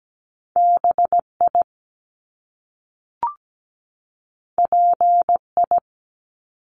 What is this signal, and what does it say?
Morse code, heard as a keyed tone, about 17 wpm, 700 Hz. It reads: BI PI
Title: Top Two Letter Combinations in Words 17wpm